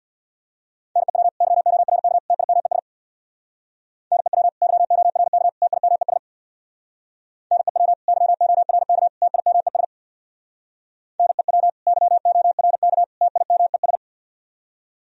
Morse code, heard as a keyed tone, about 64 wpm, 700 Hz. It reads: NEW YORK TIMES NEW YORK TIMES NEW YORK TIMES NEW YORK TIMES